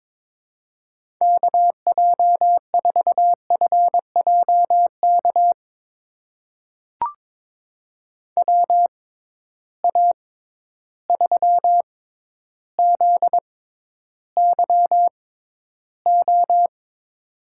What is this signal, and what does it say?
Morse code, heard as a keyed tone, about 22 wpm, spaced as for 11 wpm, 700 Hz. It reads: KJ4FJK WA3ZYO
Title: USA Call Signs - 6 Characters - 6x Character Spacing - 22wpm